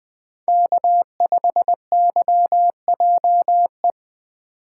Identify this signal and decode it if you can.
Morse code, heard as a keyed tone, about 20 wpm, 700 Hz. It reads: K5YJE
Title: Speed Racing - USA 1x3 Call Signs 17wpm